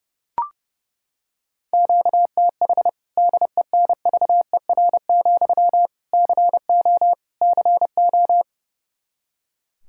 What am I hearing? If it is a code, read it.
Morse code, heard as a keyed tone, about 30 wpm, 700 Hz. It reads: QTH DENVER, CO CO
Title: Practice QSO - Exchange 1 - Beginning - 30wpm